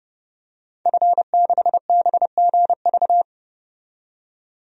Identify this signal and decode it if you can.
Morse code, heard as a keyed tone, about 31 wpm, 700 Hz. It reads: F6BGV
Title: DXCC-100 Call Signs - 5 Characters - 30wpm